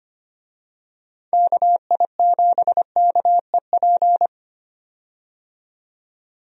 Morse code, heard as a keyed tone, about 25 wpm, 700 Hz. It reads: KI7KEP
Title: USA Call Signs - 6 Characters - 25wpm